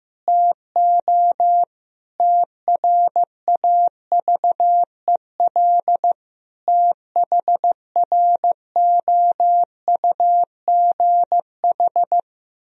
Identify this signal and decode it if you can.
Morse code, heard as a keyed tone, about 15 wpm, 700 Hz. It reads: TO TRAVEL THROUGH
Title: Sentences from Top 1000 Words - Review Part 2 - 15wpm